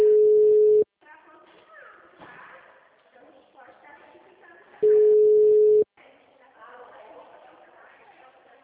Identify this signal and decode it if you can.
Ringback tone (Europe, 425 Hz), with speech.